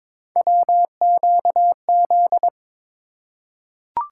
Morse code, heard as a keyed tone, about 22 wpm, 700 Hz. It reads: WQZ
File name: practice / 22wpm / Three Letter-Number Combinations - 22wpm